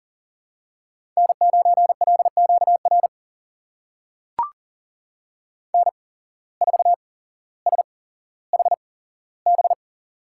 Morse code, heard as a keyed tone, about 41 wpm, spaced as for 17 wpm, 700 Hz. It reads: N9LQR N4SHB
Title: USA Call Signs - 8x Character Spacing - 40wpm